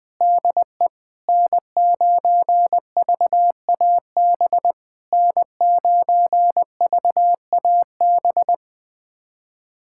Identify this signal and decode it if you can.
Morse code, heard as a keyed tone, about 20 wpm, 690 Hz. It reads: DE N9VAB N9VAB
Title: SOTA QSO Practice 20wpm